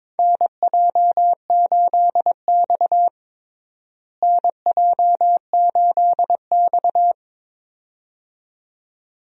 Morse code, heard as a keyed tone, about 22 wpm, 700 Hz. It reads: NJ8X NJ8X
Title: USA Call Signs - 4 Characters - Repeated 2x - 22wpm